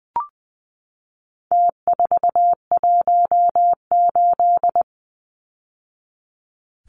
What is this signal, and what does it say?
Morse code, heard as a keyed tone, about 20 wpm, 700 Hz. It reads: T418
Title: Numbers - Contest Style - 4-Digit 20wpm